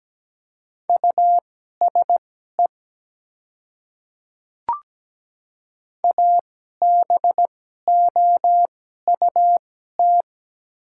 Morse code, heard as a keyed tone, about 17 wpm, spaced as for 12 wpm, 700 Hz. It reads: USE ABOUT
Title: 2x Character Spacing - Top 100 Words - 17wpm